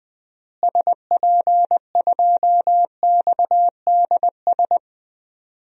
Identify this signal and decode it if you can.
Morse code, heard as a keyed tone, about 20 wpm, 700 Hz. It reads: SP2XDS
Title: DXCC-100 Call Signs - 6 Characters - Repeated 2x - 20wpm